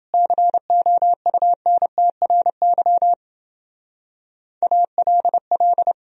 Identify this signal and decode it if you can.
Morse code, heard as a keyed tone, about 28 wpm, 700 Hz. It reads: COUNTRY ALL